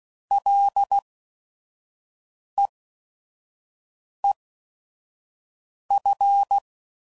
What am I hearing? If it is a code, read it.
Morse code, heard as a keyed tone, about 16 wpm, spaced as for 5 wpm, 800 Hz. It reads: LEEF